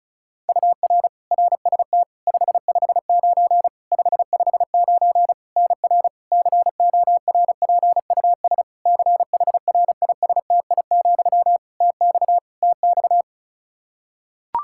Morse code, heard as a keyed tone, about 36 wpm, 700 Hz. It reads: UR RST 559 559 NR CORPUS CHRISTI, TX TX
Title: K1USN SST QSO Practice 35wpm